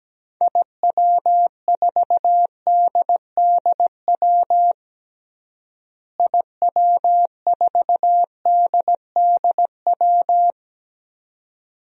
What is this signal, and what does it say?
Morse code, heard as a keyed tone, about 17 wpm, 700 Hz. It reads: IW4DDW IW4DDW
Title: DXCC-100 Call Signs - 6 Characters - Repeated 2x - 17wpm